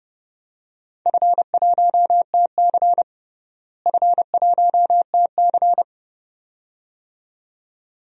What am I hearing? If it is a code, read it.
Morse code, heard as a keyed tone, about 32 wpm, 700 Hz. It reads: F1TC F1TC
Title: DXCC-100 Call Signs - 4 Characters - Repeated 2x - 30wpm